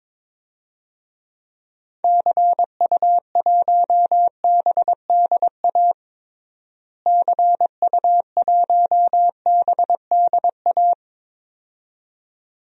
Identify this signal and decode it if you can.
Morse code, heard as a keyed tone, about 22 wpm, 700 Hz. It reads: CU1BDA CU1BDA
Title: DXCC-100 Call Signs - 6 Characters - Repeated 2x - 22wpm